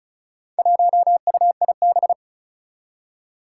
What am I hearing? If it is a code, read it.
Morse code, heard as a keyed tone, about 36 wpm, 700 Hz. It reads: 1UIB
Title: Four Challenging Letter-Number Combinations 35wpm